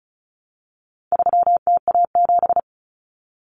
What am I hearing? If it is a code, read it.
Morse code, heard as a keyed tone, about 34 wpm, 700 Hz. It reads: 3TA7
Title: Common Cut Numbers - 4-Digit 35wpm